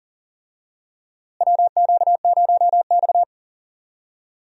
Morse code, heard as a keyed tone, about 40 wpm, 700 Hz. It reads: WQ0X
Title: USA Call Signs - 4 Characters - Repeated 3x - 40wpm